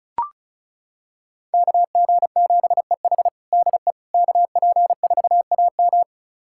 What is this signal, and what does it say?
Morse code, heard as a keyed tone, about 36 wpm, 700 Hz. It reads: KG7EH DE KP4AM
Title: USA Callsign DE Callsign - 5 Characters - 35wpm